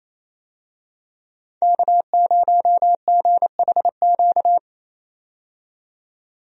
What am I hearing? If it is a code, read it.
Morse code, heard as a keyed tone, about 28 wpm, 700 Hz. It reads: K0GHQ
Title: USA Call Signs - 5 Characters - 28wpm